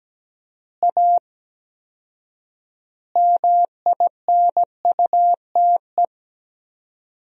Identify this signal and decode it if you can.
Morse code, heard as a keyed tone, about 17 wpm, 700 Hz. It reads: A MINUTE
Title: 4x Word-Spacing - Sentences from Top 500 Words - 17wpm